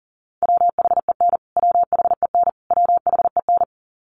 Morse code, heard as a keyed tone, about 40 wpm, 700 Hz. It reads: WHEN WHEN WHEN
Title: Top 500 Words Repeated 40wpm